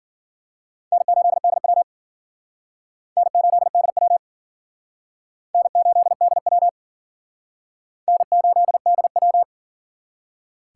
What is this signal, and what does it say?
Morse code, heard as a keyed tone, about 54 wpm, 700 Hz. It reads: N8DW N8DW N8DW N8DW